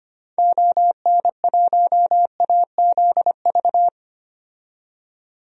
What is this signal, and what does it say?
Morse code, heard as a keyed tone, about 25 wpm, 700 Hz. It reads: ON1AZV
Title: DXCC-100 Call Signs - 6 Characters - 25wpm